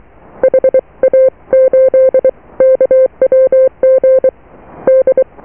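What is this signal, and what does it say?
Morse code, heard as a keyed tone, about 23 wpm, spaced as for 20 wpm, 520 Hz. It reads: HA8KWG D
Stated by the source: Morse QSO on 14 MHz with a German station, received at RST 579.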